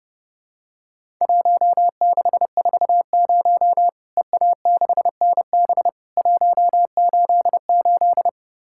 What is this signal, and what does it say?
Morse code, heard as a keyed tone, about 32 wpm, 700 Hz. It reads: 1640 EA6NB 188
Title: Worked All Europe DX Contest - QTC Exchange - 30wpm